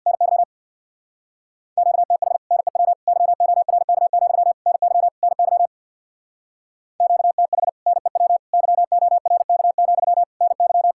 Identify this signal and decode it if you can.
Morse code, heard as a keyed tone, about 67 wpm, 700 Hz. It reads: NY QTH NEW YORK, NY NY QTH NEW YORK, NY